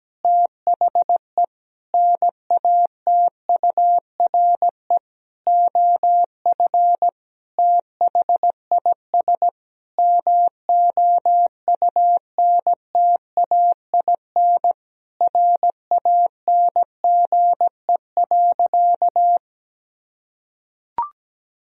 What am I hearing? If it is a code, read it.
Morse code, heard as a keyed tone, about 17 wpm, 700 Hz. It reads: THE NATURE OF THIS MOUNTAIN RANGE.
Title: Sentences from Top 1000 Words - Review Part 4 - 17wpm